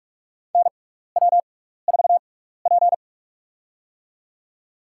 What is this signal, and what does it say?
Morse code, heard as a keyed tone, about 46 wpm, spaced as for 15 wpm, 700 Hz. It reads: NW4P
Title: USA Call Signs - 4 Characters - 6x Character Spacing - 45wpm